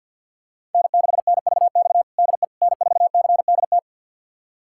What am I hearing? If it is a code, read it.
Morse code, heard as a keyed tone, about 52 wpm, 700 Hz. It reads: N6NVX DE N4CDT
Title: USA Callsign DE Callsign - 5 Characters - 50wpm